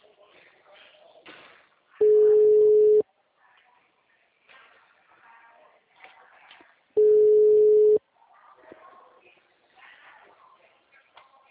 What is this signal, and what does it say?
Ringback tone (Europe, 425 Hz), with speech.
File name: out- -2011- - - .54701